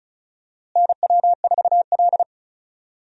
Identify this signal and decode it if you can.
Morse code, heard as a keyed tone, about 36 wpm, 700 Hz. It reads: NW4L
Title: USA Call Signs - 4 Characters - Repeated 3x - 35wpm